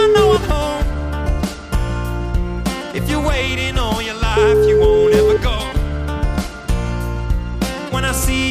Ringback tone (Europe, 425 Hz), with noise.